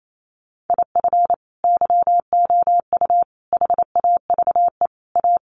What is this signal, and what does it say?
Morse code, heard as a keyed tone, about 28 wpm, 700 Hz. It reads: IF YOU HAVE A
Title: Sets of 4 Words 28wpm